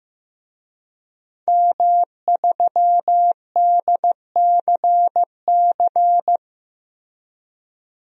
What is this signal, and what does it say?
Morse code, heard as a keyed tone, about 15 wpm, 700 Hz. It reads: M3DCC
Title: DXCC-100 Call Signs - 5 Characters - Repeated 3x - 15wpm